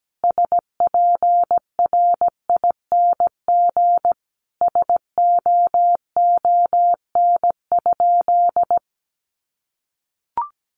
Morse code, heard as a keyed tone, about 17 wpm, 700 Hz. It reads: SPRING SOON?
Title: Sentences from Top 1200 Words 17wpm